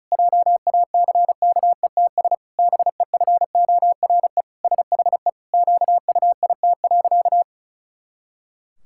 Morse code, heard as a keyed tone, about 34 wpm, 700 Hz. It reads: JACKETS BEFORE SHE QUIT.
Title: Pangram Challenge v2 - 35wpm